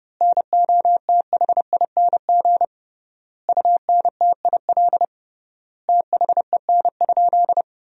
Morse code, heard as a keyed tone, about 30 wpm, 700 Hz. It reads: NOTHING UNTIL THEN?
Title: 3x Word-Spacing - Sentences from Top 500 Words - 30wpm